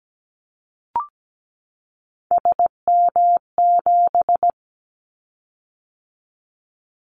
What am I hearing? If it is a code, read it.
Morse code, heard as a keyed tone, about 17 wpm, 700 Hz. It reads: SM7
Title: Callsign Prefixes for Top 25 DXCC - 17wpm